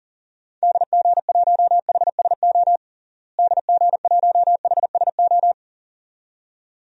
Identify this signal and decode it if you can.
Morse code, heard as a keyed tone, about 37 wpm, 700 Hz. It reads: DG1HSO DG1HSO
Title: DXCC-100 Call Signs - 6 Characters - Repeated 2x - 40wpm